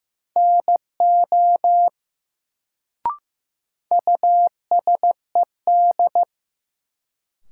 Morse code, heard as a keyed tone, about 16 wpm, 700 Hz. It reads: NO USED
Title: Sets of 2 Words - New Word Drill - 15wpm